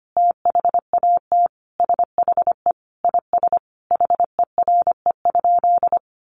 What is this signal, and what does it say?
Morse code, heard as a keyed tone, about 25 wpm, 700 Hz. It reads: THAT SHE IS HERE?
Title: Sentences from Top 1300 Words 25wpm